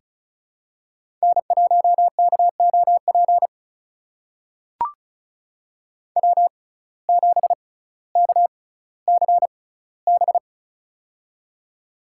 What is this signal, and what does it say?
Morse code, heard as a keyed tone, about 36 wpm, spaced as for 18 wpm, 700 Hz. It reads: N1KOP W7KCB
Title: USA Call Signs - 5 Characters - 6x Character Spacing - 35wpm